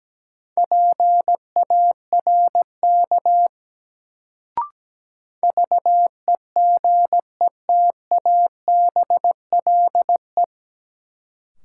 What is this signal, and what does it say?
Morse code, heard as a keyed tone, about 17 wpm, 700 Hz. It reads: PARK VEGETABLE
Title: Sentences from Top 1300 Words - New Word Drill Alt - 17wpm